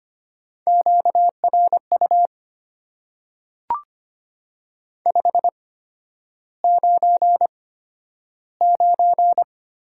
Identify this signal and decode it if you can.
Morse code, heard as a keyed tone, about 25 wpm, spaced as for 11 wpm, 690 Hz. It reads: QRU 599
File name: Top 90 QSO Elements - 8x Character Spacing - 25wpm